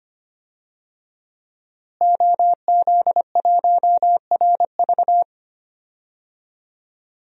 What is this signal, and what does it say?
Morse code, heard as a keyed tone, about 25 wpm, 700 Hz. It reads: OZ1RV
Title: DXCC-100 Call Signs - 5 Characters - Repeated 2x - 25wpm